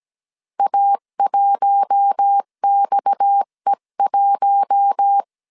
Morse code, heard as a keyed tone, about 17 wpm, 800 Hz. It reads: A1XE1
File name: CW / DET / 1 6 1500C S C 16wpm 2733